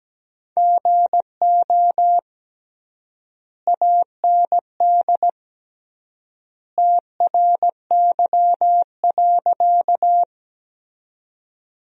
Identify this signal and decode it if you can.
Morse code, heard as a keyed tone, about 17 wpm, 700 Hz. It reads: GO AND TRY.
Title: 3x Word-Spacing - Sentences from Top 200 Words - 17wpm